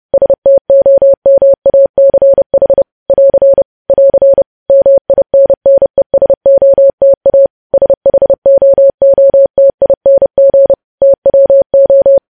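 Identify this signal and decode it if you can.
Morse code, heard as a keyed tone, about 30 wpm, 550 Hz. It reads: STOMACH <AR> <AR> MINNESOTA SHOOTING TWO
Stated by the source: Morse code practice of news headlines sent at 30 wpm.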